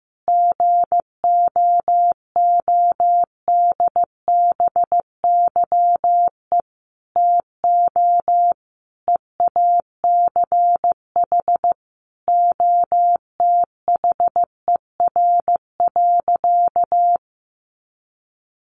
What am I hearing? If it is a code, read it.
Morse code, heard as a keyed tone, about 14 wpm, 700 Hz. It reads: GOODBYE TO EACH OTHER.
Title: Additional Easy Phrases 15wpm